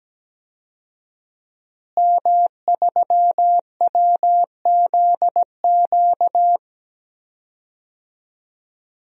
Morse code, heard as a keyed tone, about 17 wpm, 700 Hz. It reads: M3WZQ
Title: DXCC-100 Call Signs - 5 Characters - Repeated 2x - 17wpm